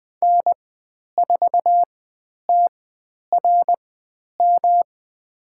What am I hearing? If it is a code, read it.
Morse code, heard as a keyed tone, about 20 wpm, spaced as for 10 wpm, 700 Hz. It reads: N4TRM
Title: 109 Lesson 18g - Letter m - Call Signs 20wpm